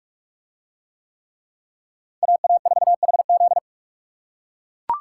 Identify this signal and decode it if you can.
Morse code, heard as a keyed tone, about 39 wpm, 700 Hz. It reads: AA4HZ